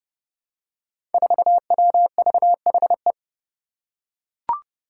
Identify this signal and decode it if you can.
Morse code, heard as a keyed tone, about 30 wpm, 700 Hz. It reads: 4WVHE
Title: Five Challenging Letter-Number Combinations 30wpm